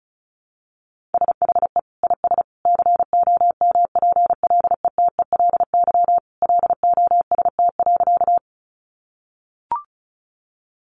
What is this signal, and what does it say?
Morse code, heard as a keyed tone, about 34 wpm, 700 Hz. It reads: SHE IS COMPLETELY LOST.
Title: Additional Easy Phrases 35wpm